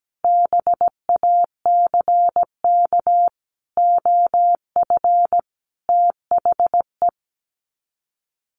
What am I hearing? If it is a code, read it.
Morse code, heard as a keyed tone, about 17 wpm, 700 Hz. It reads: BACK OF THE
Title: Sets of 5 Words 17wpm